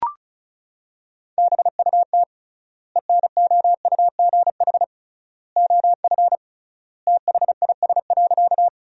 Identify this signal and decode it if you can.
Morse code, heard as a keyed tone, about 34 wpm, 700 Hz. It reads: BUT ENOUGH OF THIS.
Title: 3x Word-Spacing - Sentences from Top 400 Words - 35wpm